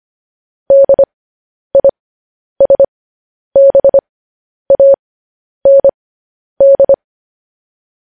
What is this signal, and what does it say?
Morse code, heard as a keyed tone, about 25 wpm, spaced as for 10 wpm, 550 Hz. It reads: DISBAND
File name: podcast / News headlines-1x-25x10